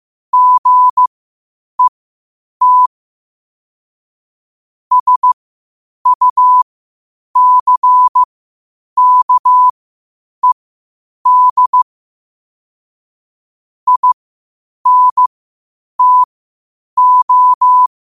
Morse code, heard as a keyed tone, about 15 wpm, spaced as for 8 wpm, 1000 Hz. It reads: GET SUCKED INTO
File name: qotdfiles / 8wpm